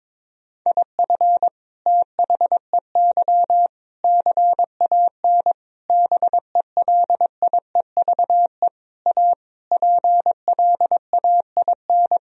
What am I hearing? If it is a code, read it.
Morse code, heard as a keyed tone, about 22 wpm, 700 Hz. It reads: IF THEY CAN BELIEVE A PLAIN
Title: Sentences from Top 1000 Words - Review Part 2 - 22wpm